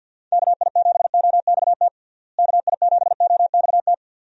Morse code, heard as a keyed tone, about 51 wpm, 700 Hz. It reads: KI7OXT KI7OXT
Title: USA Call Signs - 6 Characters - Repeated 2x - 50wpm